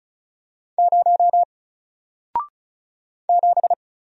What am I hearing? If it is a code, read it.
Morse code, heard as a keyed tone, about 36 wpm, 700 Hz. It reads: 0 7
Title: Single Letter-Number Alt 35wpm